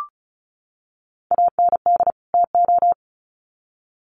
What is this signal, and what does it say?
Morse code, heard as a keyed tone, about 34 wpm, 700 Hz. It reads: AND TO
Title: Sets of 2 Words 35wpm